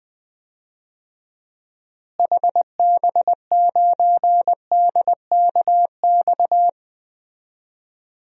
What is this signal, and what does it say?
Morse code, heard as a keyed tone, about 20 wpm, 700 Hz. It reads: HB9DKX
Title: DXCC-100 Call Signs - 6 Characters - Repeated 3x - 20wpm